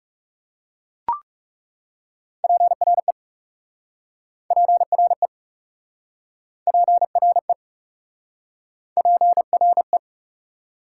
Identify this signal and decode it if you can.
Morse code, heard as a keyed tone, about 40 wpm, 690 Hz. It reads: PRE PRE PRE PRE